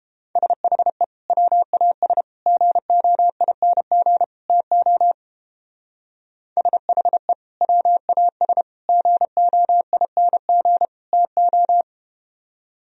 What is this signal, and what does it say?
Morse code, heard as a keyed tone, about 31 wpm, 700 Hz. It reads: SHE WAS GOING TO SHE WAS GOING TO